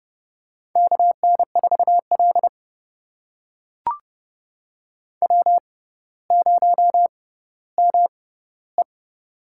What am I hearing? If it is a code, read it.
Morse code, heard as a keyed tone, about 30 wpm, spaced as for 15 wpm, 690 Hz. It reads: KN4L W0ME
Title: USA Call Signs - 4 Characters - 6x Character Spacing - 30wpm